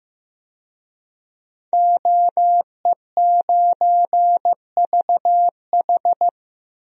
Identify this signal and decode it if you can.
Morse code, heard as a keyed tone, about 14 wpm, 700 Hz. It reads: OE9VH
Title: DXCC-100 Call Signs - 5 Characters - Repeated 3x - 15wpm